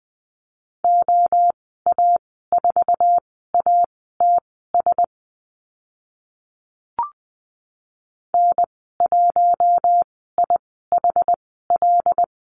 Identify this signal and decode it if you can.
Morse code, heard as a keyed tone, about 20 wpm, spaced as for 15 wpm, 700 Hz. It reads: OA4ATS N1IHL
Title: 082 Lesson 14g - Number 2 - Call Signs 20wpm